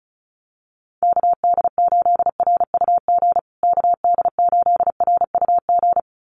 Morse code, heard as a keyed tone, about 34 wpm, 700 Hz. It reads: KD8RUG KD8RUG
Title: USA Call Signs Repeated 2x 35wpm